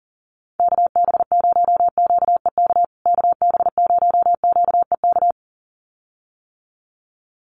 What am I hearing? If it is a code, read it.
Morse code, heard as a keyed tone, about 40 wpm, 700 Hz. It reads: KB0QEK KB0QEK
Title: USA Call Signs Repeated 2x 40wpm